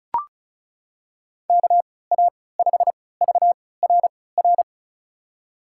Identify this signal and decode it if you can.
Morse code, heard as a keyed tone, about 36 wpm, spaced as for 20 wpm, 700 Hz. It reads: KA5VRR
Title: USA Call Signs - 6 Characters - 3x Character Spacing - 35wpm